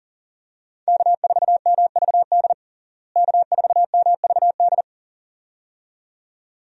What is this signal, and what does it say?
Morse code, heard as a keyed tone, about 37 wpm, 700 Hz. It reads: K4MVD K4MVD
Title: USA Call Signs - 5 Characters - Repeated 2x - 40wpm